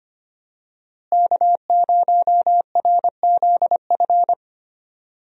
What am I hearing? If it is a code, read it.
Morse code, heard as a keyed tone, about 25 wpm, 700 Hz. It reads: K0RZF
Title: USA Call Signs - 5 Characters - Repeated 2x - 25wpm